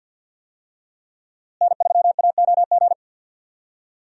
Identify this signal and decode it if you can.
Morse code, heard as a keyed tone, about 46 wpm, 700 Hz. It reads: N3AOG
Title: USA Call Signs - 5 Characters - Repeated 2x - 50wpm